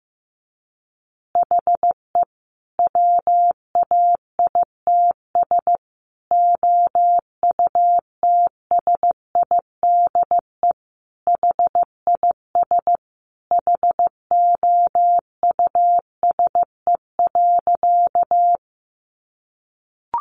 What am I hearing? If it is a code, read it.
Morse code, heard as a keyed tone, about 15 wpm, 700 Hz. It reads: HE WAITS OUTSIDE HIS HOUSE.
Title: Additional Easy Phrases 15wpm